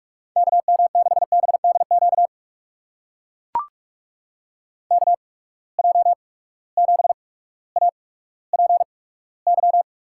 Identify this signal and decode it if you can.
Morse code, heard as a keyed tone, about 44 wpm, spaced as for 19 wpm, 700 Hz. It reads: KM6BDQ KJ7APY
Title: USA Call Signs - 6 Characters - 8x Character Spacing - 45wpm